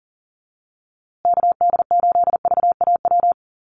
Morse code, heard as a keyed tone, about 40 wpm, 700 Hz. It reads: KD8VAW
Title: USA Call Signs Repeated 3x 40wpm